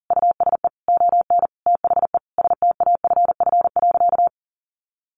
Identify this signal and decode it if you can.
Morse code, heard as a keyed tone, about 38 wpm, 700 Hz. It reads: USE ON THE STAFF.
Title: Sentences from Top 1200 Words 40wpm